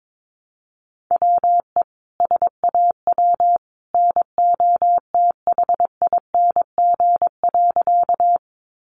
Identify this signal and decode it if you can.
Morse code, heard as a keyed tone, about 22 wpm, 700 Hz. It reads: WE SAW NOTHING.
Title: A Sound Of Thunder 22wpm